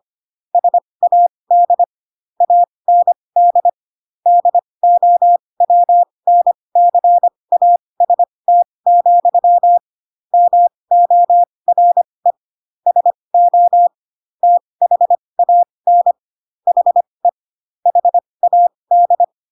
Morse code, heard as a keyed tone, about 25 wpm, spaced as for 20 wpm, 700 Hz. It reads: SAD AND DOWNCAST, MORE SO THAN HE HAD
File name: Pinocchio / Pinocchio ch3 15 25 ik1hge